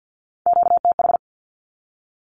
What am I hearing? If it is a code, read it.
Morse code, heard as a keyed tone, about 50 wpm, 700 Hz. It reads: QTH
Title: Q-Codes 50wpm